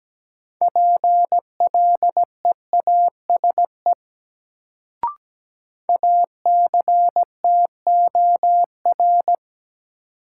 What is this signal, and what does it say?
Morse code, heard as a keyed tone, about 17 wpm, 700 Hz. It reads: PLEASE ACTOR
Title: Sentences from Top 1500 Words - New Word Drill Alt - 17wpm